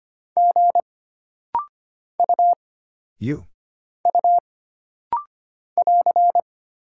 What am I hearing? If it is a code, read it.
Morse code, heard as a keyed tone, about 25 wpm, 700 Hz. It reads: G U U <AR>
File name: Single Character (Letter:Number:Punctuation:Prosign) - Rapid-Fire - 25wpm